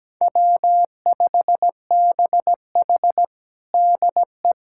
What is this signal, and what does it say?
Morse code, heard as a keyed tone, about 17 wpm, 700 Hz. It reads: W5BH DE
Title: USA Callsign DE Callsign - 4 Characters - 17wpm